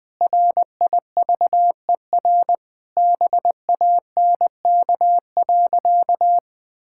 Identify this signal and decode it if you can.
Morse code, heard as a keyed tone, about 20 wpm, 700 Hz. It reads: RIVER BANK.
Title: Sentences from Top 1000 Words - Review Part 6 - 20wpm